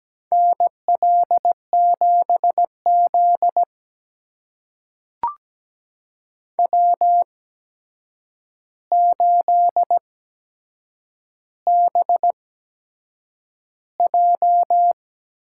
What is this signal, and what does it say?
Morse code, heard as a keyed tone, about 17 wpm, spaced as for 7 wpm, 700 Hz. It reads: NL7Z W8BJ
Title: USA Call Signs - 4 Characters - 8x Character Spacing - 17wpm